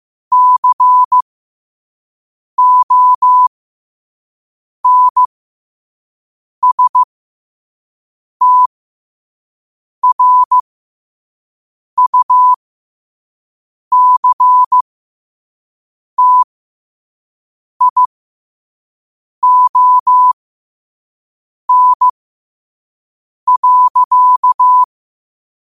Morse code, heard as a keyed tone, about 15 wpm, spaced as for 5 wpm, 1000 Hz. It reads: CONSTRUCTION.
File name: qotdfiles / 5wpm